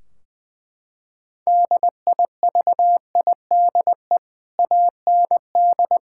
Morse code, heard as a keyed tone, about 21 wpm, 700 Hz. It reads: DIVIDE AND
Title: Sentences from Top 1600 Words 20wpm